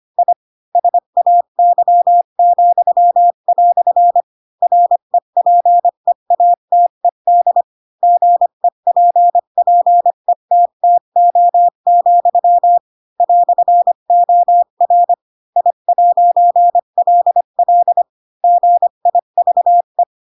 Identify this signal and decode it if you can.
Morse code, heard as a keyed tone, about 25 wpm, 700 Hz. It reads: I SAY," REPEATED GEPPETTO, "OR I'LL GIVE
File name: Pinocchio / Pinocchio ch7 20 25 ik1hge